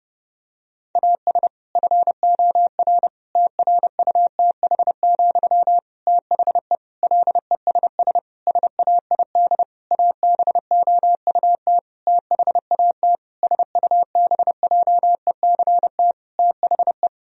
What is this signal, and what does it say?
Morse code, heard as a keyed tone, about 30 wpm, 700 Hz. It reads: AS FOR TRUTH, THE LESS SAID ABOUT THAT SUBJECT THE
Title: Sentences from Top 1000 Words - Review Part 2 - 30wpm